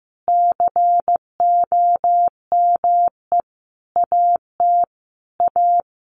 Morse code, heard as a keyed tone, about 15 wpm, 700 Hz. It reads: COME AT A
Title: Sentences from Top 900 Words 15wpm